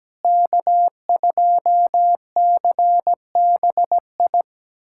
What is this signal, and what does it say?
Morse code, heard as a keyed tone, about 17 wpm, 700 Hz. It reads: K2CBI
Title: USA Callsign DE Callsign - 5 Characters - 17wpm